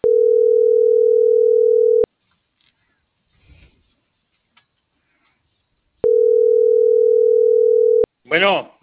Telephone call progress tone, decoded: ringback tone

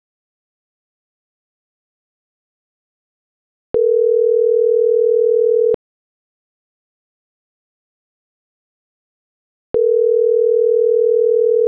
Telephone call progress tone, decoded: ringback tone

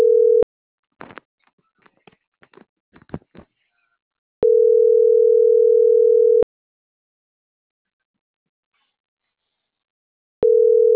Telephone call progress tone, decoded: ringback tone

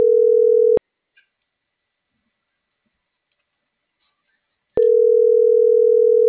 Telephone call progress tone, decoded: ringback tone

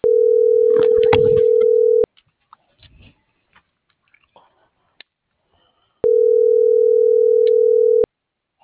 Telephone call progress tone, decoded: ringback tone

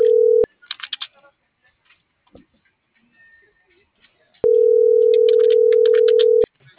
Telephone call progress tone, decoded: ringback tone

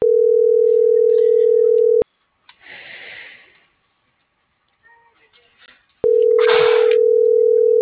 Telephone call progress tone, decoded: ringback tone

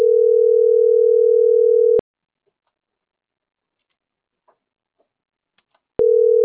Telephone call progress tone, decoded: ringback tone

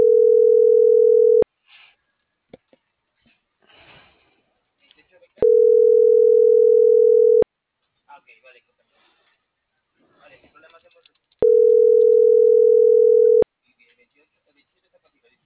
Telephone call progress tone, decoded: ringback tone

